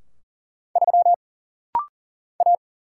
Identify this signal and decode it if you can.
Morse code, heard as a keyed tone, about 40 wpm, 700 Hz. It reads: 3 A
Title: Single Letter-Number Alt - Rapid-Fire 40wpm